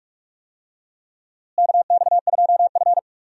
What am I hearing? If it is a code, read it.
Morse code, heard as a keyed tone, about 46 wpm, 700 Hz. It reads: KX2F
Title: USA Call Signs - 4 Characters - Repeated 2x - 45wpm